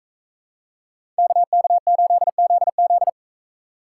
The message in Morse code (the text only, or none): KK8ZZ